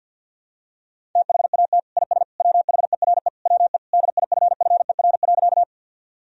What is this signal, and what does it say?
Morse code, heard as a keyed tone, about 52 wpm, 700 Hz. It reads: THAT IS WHERE WE DIFFER.
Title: Sentences from Top 1000 Words - Review Part 1 - 50wpm